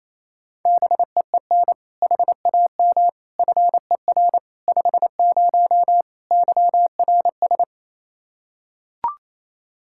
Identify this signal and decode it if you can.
Morse code, heard as a keyed tone, about 27 wpm, 700 Hz. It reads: BEEN HAM FER 50 YRS
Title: K1USN SST QSO Practice 28wpm